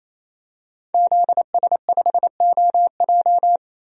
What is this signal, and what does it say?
Morse code, heard as a keyed tone, about 28 wpm, 700 Hz. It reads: ZS5OJ
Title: DXCC-100 Call Signs - 5 Characters - 28wpm